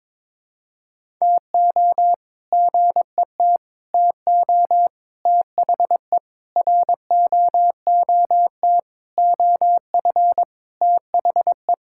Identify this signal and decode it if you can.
Morse code, heard as a keyed tone, about 21 wpm, 700 Hz. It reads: TO GET TO THE ROOT OF THE
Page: morse-code-ninja-practice-22wpm